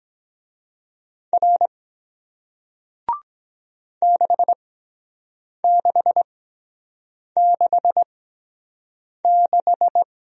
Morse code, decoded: R 6666

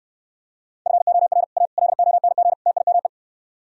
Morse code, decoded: FOR A LONG TIME

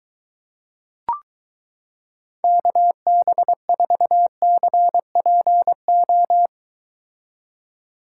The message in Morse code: KB4CPO

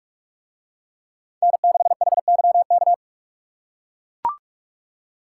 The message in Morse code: N6HYK

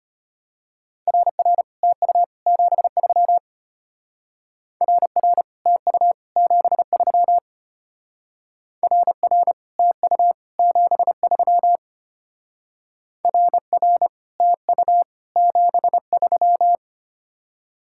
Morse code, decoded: RR TU 73 RR TU 73 RR TU 73 RR TU 73

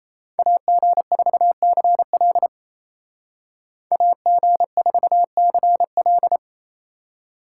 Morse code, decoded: AG4CL AG4CL

